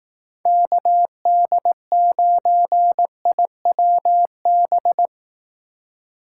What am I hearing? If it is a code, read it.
Morse code, heard as a keyed tone, about 18 wpm, 700 Hz. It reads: KD9IWB
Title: Speed Racing - USA Call Signs 15wpm